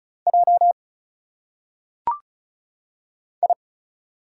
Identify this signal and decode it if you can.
Morse code, heard as a keyed tone, about 36 wpm, 700 Hz. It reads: J I
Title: Single Letters - Phonetic Pronunciation 35wpm